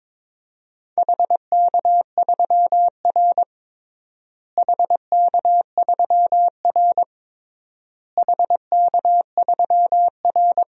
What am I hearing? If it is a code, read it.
Morse code, heard as a keyed tone, about 22 wpm, 700 Hz. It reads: HK3R HK3R HK3R
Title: DXCC-100 Call Signs - 4 Characters - Repeated 3x - 22wpm